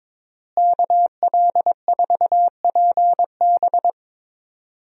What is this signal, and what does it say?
Morse code, heard as a keyed tone, about 22 wpm, 700 Hz. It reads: KL4PB
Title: USA Call Signs - 5 Characters - Repeated 3x - 22wpm